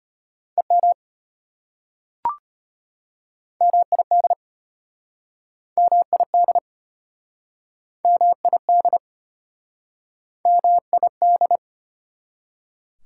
Morse code, heard as a keyed tone, about 32 wpm, 690 Hz. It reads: EM MID MID MID MID